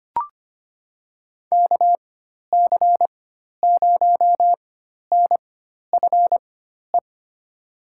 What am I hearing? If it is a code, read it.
Morse code, heard as a keyed tone, about 25 wpm, spaced as for 12 wpm, 700 Hz. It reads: KC0NFE
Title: USA Call Signs - 6 Characters - 4x Character Spacing - 25wpm